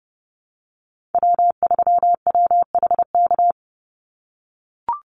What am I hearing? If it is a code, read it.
Morse code, heard as a keyed tone, about 30 wpm, 700 Hz. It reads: W3WHK
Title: Call Signs - 30wpm